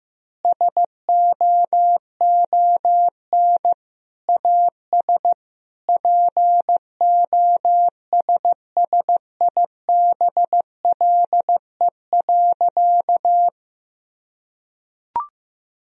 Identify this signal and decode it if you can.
Morse code, heard as a keyed tone, about 15 wpm, 700 Hz. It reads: SOON AS POSSIBLE.
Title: Sentences from Top 1000 Words - Review Part 4 - 15wpm